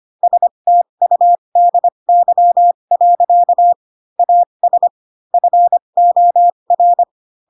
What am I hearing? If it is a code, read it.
Morse code, heard as a keyed tone, about 25 wpm, spaced as for 22 wpm, 700 Hz. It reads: STUDY. AS FOR
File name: Pinocchio / Pinocchio ch4 18 25 ik1hge